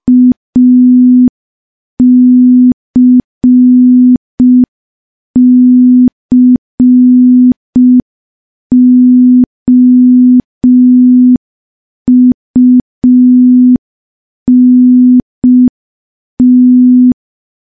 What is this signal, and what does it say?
Morse code, heard as a keyed tone, about 5 wpm, 250 Hz. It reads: ACCOUNT
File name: morse / morse-training-05WPM